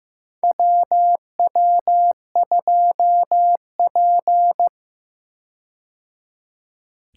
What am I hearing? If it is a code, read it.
Morse code, heard as a keyed tone, about 15 wpm, 700 Hz. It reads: WW2P
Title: K1USN SST QSO Practice - 15wpm